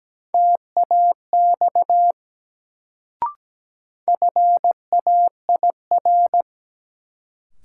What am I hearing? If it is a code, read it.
Morse code, heard as a keyed tone, about 17 wpm, 700 Hz. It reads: TAX FAIR
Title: Sentences from Top 1000 Words - Review Part 5 - New Word Drill - 17wpm